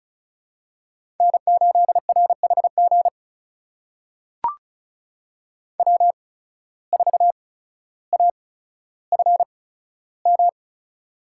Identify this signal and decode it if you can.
Morse code, heard as a keyed tone, about 36 wpm, spaced as for 15 wpm, 700 Hz. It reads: N8RHG W4AFM
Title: USA Call Signs - 5 Characters - 8x Character Spacing - 35wpm